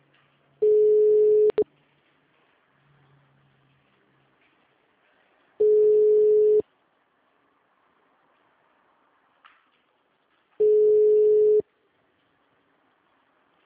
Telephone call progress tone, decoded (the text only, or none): ringback tone